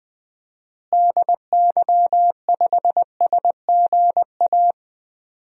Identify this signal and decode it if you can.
Morse code, heard as a keyed tone, about 20 wpm, 700 Hz. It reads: DY5SGA
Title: DXCC-100 Call Signs - 6 Characters - Repeated 2x - 20wpm